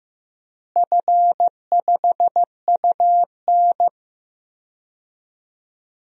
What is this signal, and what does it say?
Morse code, heard as a keyed tone, about 15 wpm, 700 Hz. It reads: F5UN
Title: DXCC-100 Call Signs - 4 Characters - Repeated 2x - 15wpm